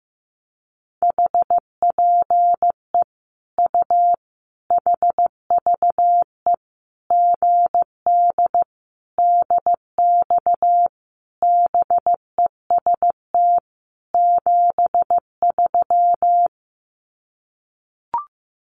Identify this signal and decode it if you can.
Morse code, heard as a keyed tone, about 15 wpm, 700 Hz. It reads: HPE U HVE GD DX BEST 73
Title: Common QSO Phrases 15wpm